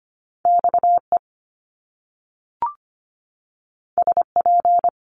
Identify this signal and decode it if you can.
Morse code, heard as a keyed tone, about 25 wpm, 700 Hz. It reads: X E S P
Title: Top Two Letter Combinations in Words 25wpm